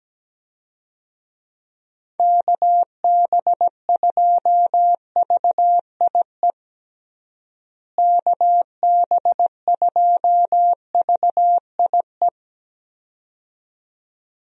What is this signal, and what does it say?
Morse code, heard as a keyed tone, about 17 wpm, 700 Hz. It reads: KB2VIE KB2VIE
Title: USA Call Signs - 6 Characters - Repeated 2x - 17wpm